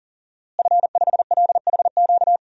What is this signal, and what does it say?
Morse code, heard as a keyed tone, about 40 wpm, 700 Hz. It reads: F5LHQ
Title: DXCC-100 Call Signs - 5 Characters - Repeated 3x - 40wpm